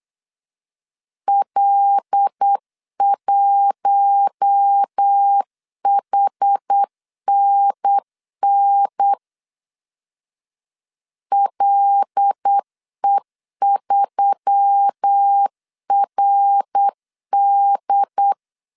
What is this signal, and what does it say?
Morse code, heard as a keyed tone, about 8 wpm, 800 Hz. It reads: L1HNN LE3RD
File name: 8wpm